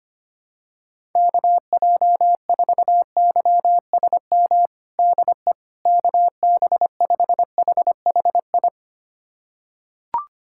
Morse code, heard as a keyed tone, about 25 wpm, 700 Hz. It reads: KJ4YSM DE KB5HHI
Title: USA Callsign DE Callsign - 6 Characters - 25wpm